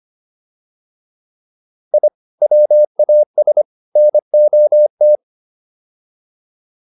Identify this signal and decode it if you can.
Morse code, heard as a keyed tone, about 25 wpm, 600 Hz. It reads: I WAS NOT